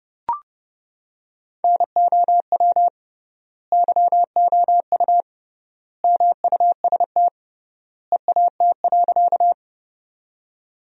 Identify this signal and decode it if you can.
Morse code, heard as a keyed tone, about 32 wpm, 700 Hz. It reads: NOW YOU MUST EAT.
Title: 3x Word-Spacing - Sentences from Top 300 Words - 30wpm